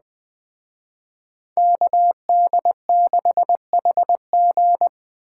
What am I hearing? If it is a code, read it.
Morse code, heard as a keyed tone, about 20 wpm, 700 Hz. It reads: KD6HG
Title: Speed Racing - USA 2x2 Call Signs 17wpm